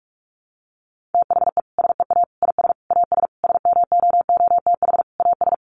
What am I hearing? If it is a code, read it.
Morse code, heard as a keyed tone, about 44 wpm, 700 Hz. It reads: THE SEA IS AS SMOOTH AS
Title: Sentences from Top 1400 Words 45wpm